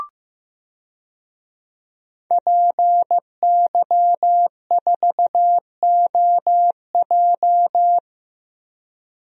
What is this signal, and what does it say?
Morse code, heard as a keyed tone, about 15 wpm, 700 Hz. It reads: PY4OJ
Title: DXCC-100 Call Signs - 5 Characters - Repeated 3x - 15wpm